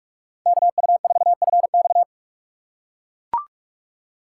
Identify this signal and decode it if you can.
Morse code, heard as a keyed tone, about 43 wpm, 700 Hz. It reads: KU4FX
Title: USA Call Signs - 6x Character Spacing - 45wpm